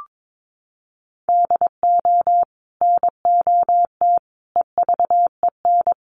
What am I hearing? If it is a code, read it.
Morse code, heard as a keyed tone, about 22 wpm, 700 Hz. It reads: DO NOT EVEN
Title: Sets of 4 Words 22wpm